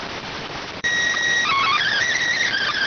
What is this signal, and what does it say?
SSTV (slow-scan television), PD 290.
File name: findings / S ss-portrait.gen-20 rx